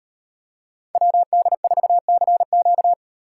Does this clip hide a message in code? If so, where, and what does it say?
Yes: Morse code from 0.9 to 2.9 s, heard as a keyed tone, about 35 wpm, 700 Hz. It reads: WD4CQ